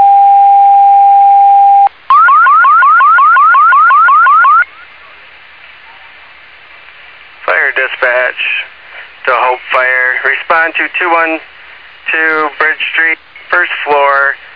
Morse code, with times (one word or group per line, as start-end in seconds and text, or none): none